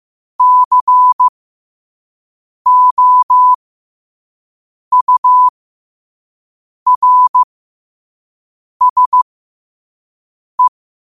Morse code, heard as a keyed tone, about 15 wpm, spaced as for 5 wpm, 1000 Hz. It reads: COURSE
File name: qotdfiles / 5wpm